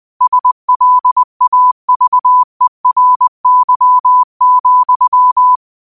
Morse code, heard as a keyed tone, about 20 wpm, 1000 Hz. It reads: SLAVERY,
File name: qotdfiles / 20wpm